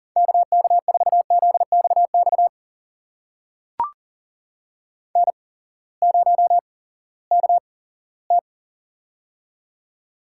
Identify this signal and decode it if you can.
Morse code, heard as a keyed tone, about 40 wpm, spaced as for 20 wpm, 700 Hz. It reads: KK4ZXX N0KT
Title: USA Call Signs - 8x Character Spacing - 40wpm